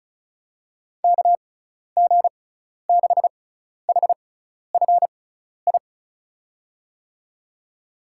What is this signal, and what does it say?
Morse code, heard as a keyed tone, about 34 wpm, spaced as for 12 wpm, 690 Hz. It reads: KG6HFI
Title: USA Call Signs - 6 Characters - 6x Character Spacing - 35wpm